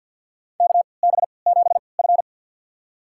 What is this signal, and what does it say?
Morse code, heard as a keyed tone, about 52 wpm, spaced as for 28 wpm, 700 Hz. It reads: KB7F
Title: USA Call Signs - 4 Characters - 3x Character Spacing - 50wpm